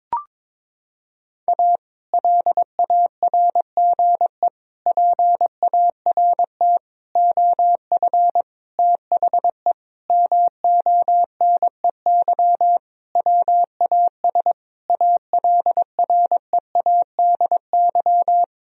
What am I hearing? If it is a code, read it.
Morse code, heard as a keyed tone, about 22 wpm, 700 Hz. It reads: A LARGE PART OF THE MONEY WAS ALREADY
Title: Sentences from Top 1000 Words - Review Part 1 - 22wpm